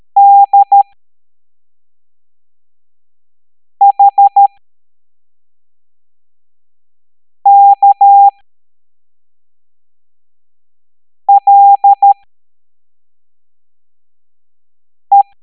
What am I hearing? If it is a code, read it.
Morse code, heard as a keyed tone, about 13 wpm, spaced as for 3 wpm, 800 Hz. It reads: DHKLE
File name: hamdata / L09-EX04